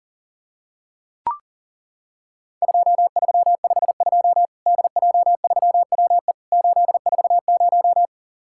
Morse code, heard as a keyed tone, about 40 wpm, 700 Hz. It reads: 2352 DJ3WE 840
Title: Worked All Europe DX Contest - QTC Exchange - 40wpm